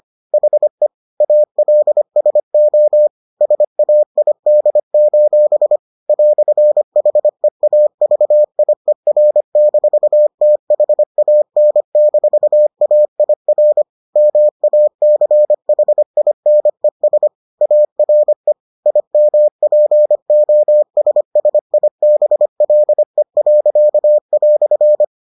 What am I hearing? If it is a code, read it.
Morse code, heard as a keyed tone, about 25 wpm, 600 Hz. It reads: HE ALSO SAID: "HEAVIER-THAN-AIR MACHINES ARE IMPOSSIBLE."